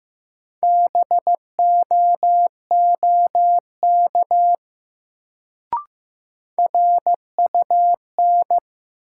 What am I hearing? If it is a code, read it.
Morse code, heard as a keyed tone, about 15 wpm, 700 Hz. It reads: BOOK RUN
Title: Sentences from Top 300 Words - New Word Drill - 15wpm